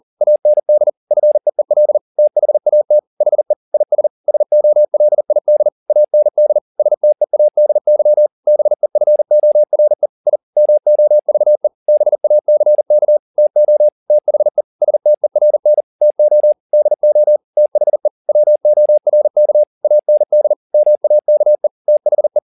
Morse code, heard as a keyed tone, about 40 wpm, 600 Hz. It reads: AND FEEL THAT HE IS SOLID AND STEADY BEFORE I MOVE BACK TO THE STERN TO DO THE WORK AND MAKE THE
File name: the old man and the sea 2 40wpm